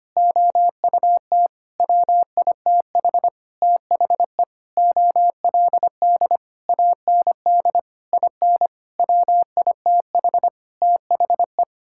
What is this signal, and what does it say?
Morse code, heard as a keyed tone, about 25 wpm, 700 Hz. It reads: OUT WITH THE OLD AND IN WITH THE
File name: practice / 25wpm / Sentences with 9-Word Phrases - 25wpm